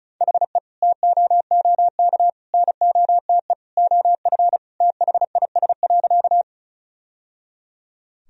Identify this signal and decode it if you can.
Morse code, heard as a keyed tone, about 36 wpm, 700 Hz. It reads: HE TOOK NOTE OF THIS.
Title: Sentences from Top 1000 Words - Review Part 2 - 35wpm